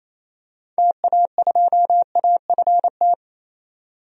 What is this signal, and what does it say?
Morse code, heard as a keyed tone, about 28 wpm, 700 Hz. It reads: TA2AFT
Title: DXCC-100 Call Signs - 6 Characters - 28wpm